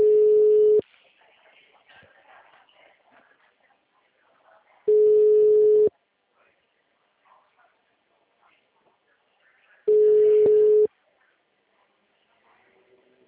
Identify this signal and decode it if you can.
Ringback tone (Europe, 425 Hz), with speech.
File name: out- -2017- - - .74063